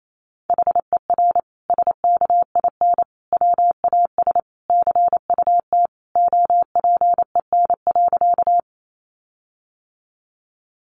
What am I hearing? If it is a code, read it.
Morse code, heard as a keyed tone, about 28 wpm, 700 Hz. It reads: HER SKIN WAS CUT OPEN.
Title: Sentences from Top 1300 Words 28wpm